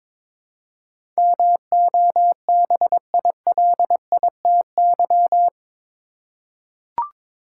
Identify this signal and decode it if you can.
Morse code, heard as a keyed tone, about 21 wpm, 700 Hz. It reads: MOBILITY
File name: practice / 22wpm / Eight Letter Words - 22wpm